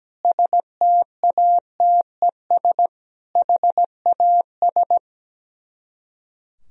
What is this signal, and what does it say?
Morse code, heard as a keyed tone, about 17 wpm, 700 Hz. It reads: STATES HAS
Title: Speed-Racing - Sets of 4 Words 17wpm